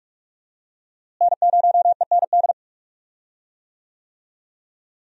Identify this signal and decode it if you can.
Morse code, heard as a keyed tone, about 46 wpm, 700 Hz. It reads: N0END
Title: USA Call Signs - 5 Characters - Repeated 3x - 45wpm